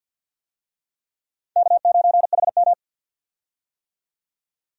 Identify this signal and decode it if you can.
Morse code, heard as a keyed tone, about 53 wpm, 700 Hz. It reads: K9HM
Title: USA Call Signs - 4 Characters - 50wpm